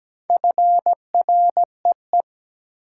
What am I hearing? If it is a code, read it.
Morse code, heard as a keyed tone, about 17 wpm, 700 Hz. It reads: FREE
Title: Sets of 5 Words Encore - New Word Drill - 17wpm